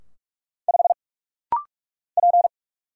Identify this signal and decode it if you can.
Morse code, heard as a keyed tone, about 46 wpm, 700 Hz. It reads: 5 P
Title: Single Letter-Number - Rapid-Fire 45wpm